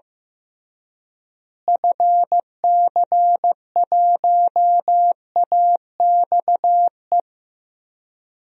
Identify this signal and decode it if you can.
Morse code, heard as a keyed tone, about 15 wpm, 700 Hz. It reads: FC1AXE
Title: DXCC-100 Call Signs - 6 Characters - Repeated 3x - 15wpm